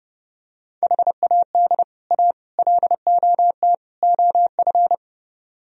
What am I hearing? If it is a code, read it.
Morse code, heard as a keyed tone, about 30 wpm, 700 Hz. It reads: HAD A LOT OF